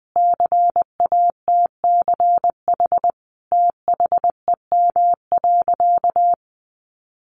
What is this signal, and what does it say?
Morse code, heard as a keyed tone, about 20 wpm, 700 Hz. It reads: CATCH THEM.
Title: Sentences from Top 1500 Words 20wpm